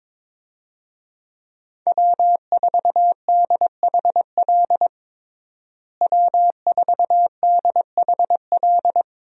Morse code, heard as a keyed tone, about 22 wpm, 700 Hz. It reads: W4DHL W4DHL
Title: USA Call Signs - 5 Characters - Repeated 2x - 22wpm